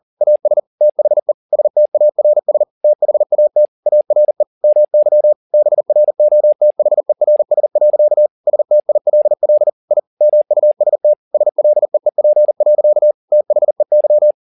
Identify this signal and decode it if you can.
Morse code, heard as a keyed tone, about 40 wpm, 600 Hz. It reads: AS THE STARS THAT ARE MY BROTHERS. STILL I MUST SLEEP. THEY
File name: the old man and the sea 2 40wpm